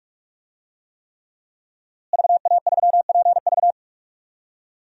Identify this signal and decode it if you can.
Morse code, heard as a keyed tone, about 43 wpm, 700 Hz. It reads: VA3PV
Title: DXCC-100 Call Signs - 5 Characters - Repeated 3x - 45wpm